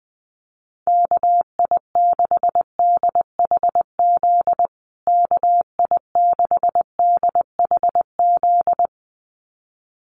Morse code, heard as a keyed tone, about 20 wpm, 700 Hz. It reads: KI6DHZ KI6DHZ
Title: USA Call Signs Repeated 2x 20wpm